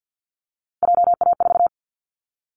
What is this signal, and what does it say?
Morse code, heard as a keyed tone, about 50 wpm, 700 Hz. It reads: JA4
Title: Callsign Prefixes for Top 50 DXCC - 50wpm